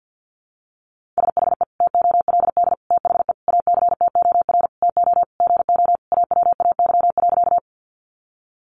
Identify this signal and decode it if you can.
Morse code, heard as a keyed tone, about 48 wpm, 700 Hz. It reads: SHE TOLD THE ACTOR TO GO AWAY.
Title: Sentences from Top 1500 Words 50wpm